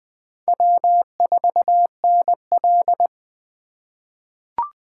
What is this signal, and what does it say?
Morse code, heard as a keyed tone, about 20 wpm, 700 Hz. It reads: W4NL
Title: USA Call Signs - 4 Characters - 2x Character Spacing - 20wpm